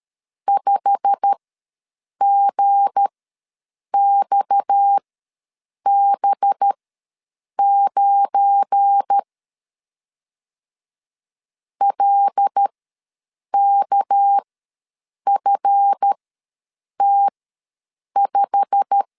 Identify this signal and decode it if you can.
Morse code, heard as a keyed tone, about 13 wpm, spaced as for 7 wpm, 800 Hz. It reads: 5GXB9 LKFT5
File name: LEZIONI / 8 YJ45 / 8 YJ45 ws 12wpm 3146